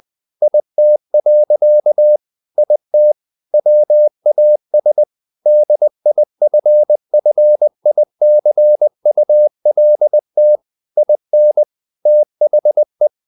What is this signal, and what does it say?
Morse code, heard as a keyed tone, about 20 wpm, 600 Hz. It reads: IT. IT WAS DIFFICULT IN THE
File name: the old man and the sea / the old man and the sea part 1 20wpm